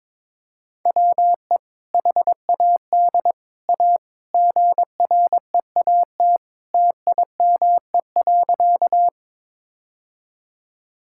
Morse code, heard as a keyed tone, about 22 wpm, 700 Hz. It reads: WE HAD A GREAT TIME.
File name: practice / 22wpm / Sentences with 2-Word Phrases - 22wpm